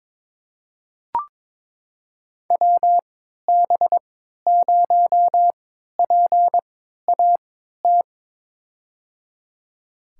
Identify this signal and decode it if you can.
Morse code, heard as a keyed tone, about 22 wpm, spaced as for 12 wpm, 700 Hz. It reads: WB0PAT
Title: USA Call Signs - 6 Characters - 3x Character Spacing - 22wpm